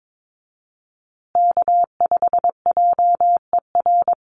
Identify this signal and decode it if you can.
Morse code, heard as a keyed tone, about 22 wpm, 700 Hz. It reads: K5JER
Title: USA Call Signs Repeated 3x 22wpm